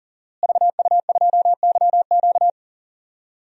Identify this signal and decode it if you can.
Morse code, heard as a keyed tone, about 40 wpm, 700 Hz. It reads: VU2YQ
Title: DXCC-100 Call Signs - 5 Characters - Repeated 3x - 40wpm